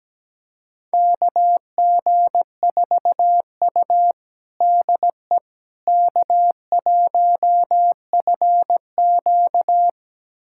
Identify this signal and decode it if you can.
Morse code, heard as a keyed tone, about 17 wpm, 700 Hz. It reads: KG4U DE K1FQ
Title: USA Callsign DE Callsign - 4 Characters - 17wpm